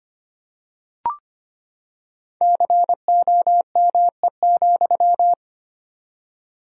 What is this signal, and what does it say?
Morse code, heard as a keyed tone, about 25 wpm, 700 Hz. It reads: COME,
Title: 4x Word-Spacing - Sentences from Top 500 Words - 25wpm